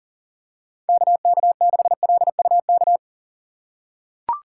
Morse code, heard as a keyed tone, about 40 wpm, 700 Hz. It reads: KK6LUK
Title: USA Call Signs - 6 Characters - 3x Character Spacing - 40wpm